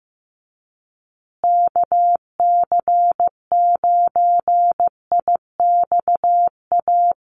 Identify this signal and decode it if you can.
Morse code, heard as a keyed tone, about 15 wpm, 700 Hz. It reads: KC9IXA
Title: USA Call Signs Repeated 3x 15wpm